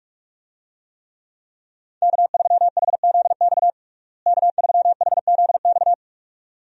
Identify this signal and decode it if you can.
Morse code, heard as a keyed tone, about 46 wpm, 700 Hz. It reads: K3HZX K3HZX
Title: USA Call Signs - 5 Characters - Repeated 2x - 45wpm